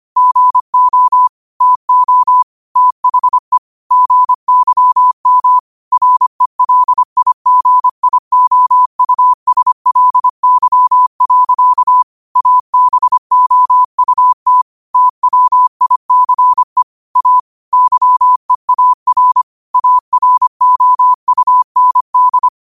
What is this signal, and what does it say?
Morse code, heard as a keyed tone, about 25 wpm, 1000 Hz. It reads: GO TO THE GYM RELIGIOUSLY. ABOUT TWICE A YEAR AROUND
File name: qotdfiles / 25wpm